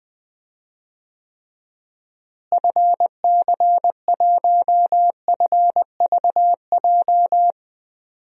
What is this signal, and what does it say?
Morse code, heard as a keyed tone, about 20 wpm, 700 Hz. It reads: FC1FVJ
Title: DXCC-100 Call Signs - 6 Characters - Repeated 2x - 20wpm